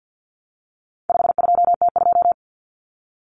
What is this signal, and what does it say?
Morse code, heard as a keyed tone, about 50 wpm, 700 Hz. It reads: 52T2
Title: Common Cut Numbers - 4-Digit 50wpm